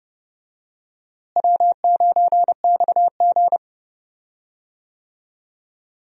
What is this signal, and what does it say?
Morse code, heard as a keyed tone, about 30 wpm, 700 Hz. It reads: W9XG
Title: USA Call Signs - 4 Characters - Repeated 2x - 30wpm